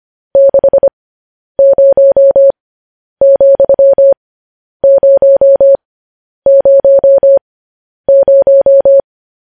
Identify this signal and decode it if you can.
Morse code, heard as a keyed tone, about 25 wpm, spaced as for 10 wpm, 550 Hz. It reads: 60,000